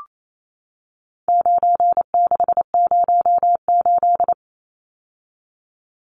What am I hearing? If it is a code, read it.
Morse code, heard as a keyed tone, about 28 wpm, 700 Hz. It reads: 9608
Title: Numbers - 4-Digit 28wpm